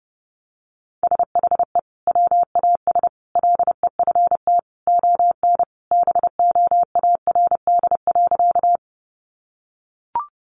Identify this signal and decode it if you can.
Morse code, heard as a keyed tone, about 30 wpm, 700 Hz. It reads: SHE WAS LEFT ON BOARD.
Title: Sentences from Top 700 Words 30wpm